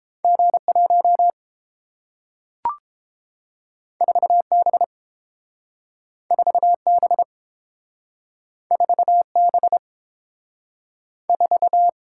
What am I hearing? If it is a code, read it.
Morse code, heard as a keyed tone, about 32 wpm, 700 Hz. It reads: G1 4B 4B 4B 4